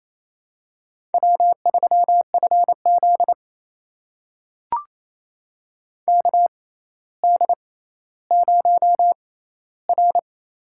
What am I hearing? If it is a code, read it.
Morse code, heard as a keyed tone, about 28 wpm, spaced as for 14 wpm, 700 Hz. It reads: W3FZ KD0R
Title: USA Call Signs - 4 Characters - 6x Character Spacing - 28wpm